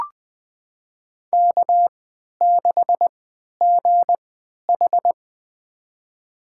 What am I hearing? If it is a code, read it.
Morse code, heard as a keyed tone, about 20 wpm, spaced as for 11 wpm, 700 Hz. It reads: K6GH
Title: USA Call Signs - 4 Characters - 3x Character Spacing - 20wpm